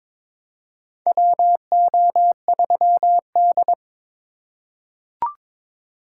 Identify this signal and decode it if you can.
Morse code, heard as a keyed tone, about 22 wpm, 700 Hz. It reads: WO3D
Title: USA Call Signs - 4 Characters - 2x Character Spacing - 22wpm